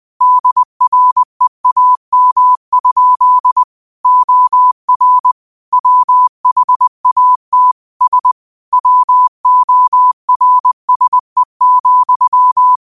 Morse code, heard as a keyed tone, about 20 wpm, 1000 Hz. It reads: DREAM? OR WHAT S WORSE,
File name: 20wpm